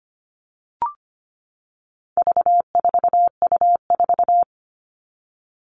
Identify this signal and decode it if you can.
Morse code, heard as a keyed tone, about 25 wpm, 700 Hz. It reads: V4U4
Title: All Cut Numbers - 4-Digit 25wpm